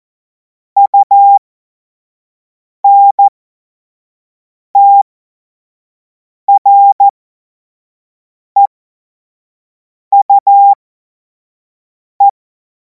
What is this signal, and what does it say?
Morse code, heard as a keyed tone, about 14 wpm, spaced as for 5 wpm, 800 Hz. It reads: UNTREUE
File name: Morsekurs / Witze / Witz 24. Februar 2026 25 F70